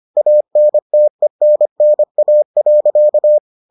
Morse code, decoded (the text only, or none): ANTENNA.